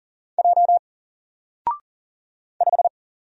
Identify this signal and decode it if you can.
Morse code, heard as a keyed tone, about 39 wpm, 700 Hz. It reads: J 5
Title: Single Character (Letter:Number:Punctuation:Prosign) 40wpm